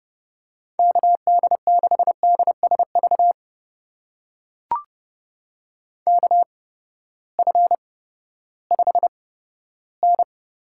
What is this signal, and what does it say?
Morse code, heard as a keyed tone, about 30 wpm, spaced as for 15 wpm, 700 Hz. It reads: KD6DSV KF5N
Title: USA Call Signs - 6 Characters - 8x Character Spacing - 30wpm